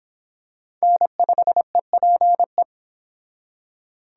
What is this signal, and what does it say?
Morse code, heard as a keyed tone, about 25 wpm, 700 Hz. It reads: N5EPE